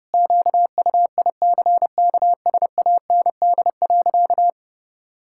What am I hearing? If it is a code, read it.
Morse code, heard as a keyed tone, about 32 wpm, 700 Hz. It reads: QUICKSAND.
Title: Pangram Challenge v2 - 30wpm